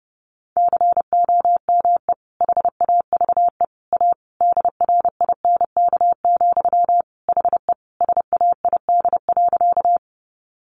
Morse code, heard as a keyed tone, about 30 wpm, 700 Hz. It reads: COME HAVE A DRINK, HE SAID.
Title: Sentences from Top 1500 Words 30wpm